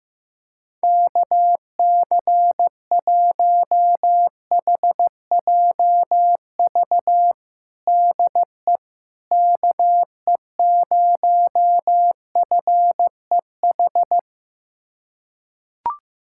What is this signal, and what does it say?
Morse code, heard as a keyed tone, about 15 wpm, 700 Hz. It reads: KC1HJV DE KE0FEH
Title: USA Callsign DE Callsign 15wpm